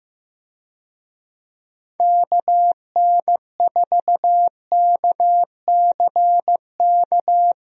Morse code, heard as a keyed tone, about 15 wpm, 700 Hz. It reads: KN4KCK
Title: USA Call Signs - 6 Characters - Repeated 2x - 15wpm